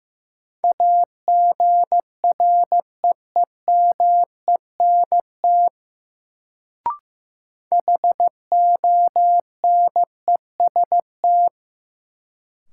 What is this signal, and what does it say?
Morse code, heard as a keyed tone, about 15 wpm, 700 Hz. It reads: AGREEMENT HONEST
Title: Sets of 5 Words Encore - New Word Drill - 15wpm